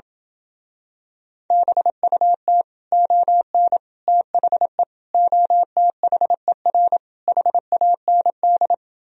Morse code, heard as a keyed tone, about 27 wpm, 700 Hz. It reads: BUT ON THE OTHER HAND